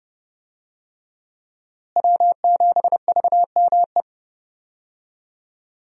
Morse code, heard as a keyed tone, about 30 wpm, 700 Hz. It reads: W7VME
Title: USA Call Signs - 5 Characters - Repeated 3x - 30wpm